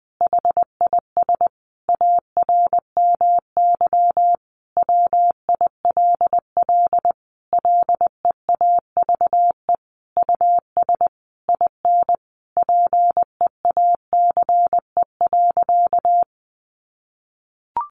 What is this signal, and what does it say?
Morse code, heard as a keyed tone, about 20 wpm, 700 Hz. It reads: HIS ARMY WILL LEAVE US IN PEACE.
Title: Sentences from Top 1500 Words 20wpm